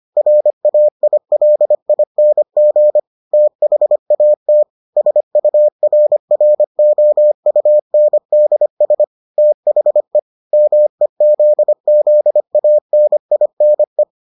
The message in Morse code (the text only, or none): RAILING THAT SURROUNDS THE MEZZANINE